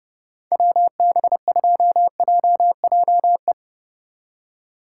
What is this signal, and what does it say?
Morse code, heard as a keyed tone, about 36 wpm, 700 Hz. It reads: WB2JJE
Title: Speed Racing - USA 2x3 Call Signs 30wpm